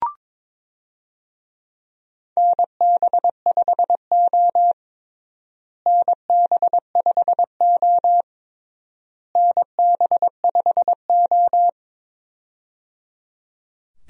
Morse code, heard as a keyed tone, about 21 wpm, 700 Hz. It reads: NB5O NB5O NB5O
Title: USA Call Signs - 4 Characters - Repeated 3x - 22wpm